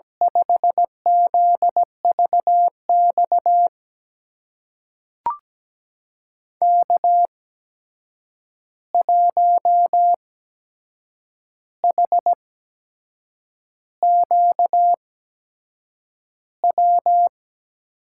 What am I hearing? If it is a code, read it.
Morse code, heard as a keyed tone, about 17 wpm, spaced as for 7 wpm, 700 Hz. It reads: 5ZVX K1HQW
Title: USA Call Signs - 8x Character Spacing - 17wpm